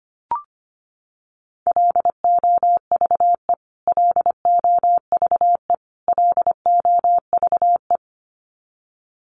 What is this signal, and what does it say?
Morse code, heard as a keyed tone, about 25 wpm, 700 Hz. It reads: LOVE LOVE LOVE
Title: Top 500 Words Repeated 25wpm